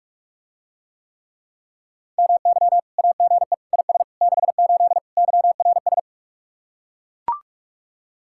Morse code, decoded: MY AGE IS 68 YRS

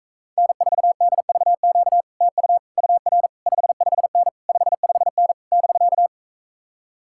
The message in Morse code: N4DVQ TU UR 55N 55N <BK>

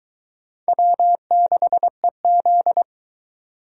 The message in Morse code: W6EZ

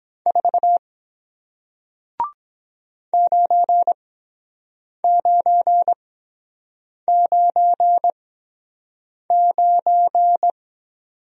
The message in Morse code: 4 9999

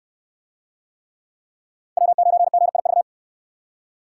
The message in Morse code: W9GIV

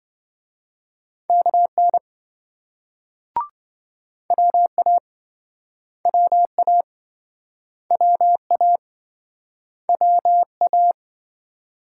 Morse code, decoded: KN WA WA WA WA

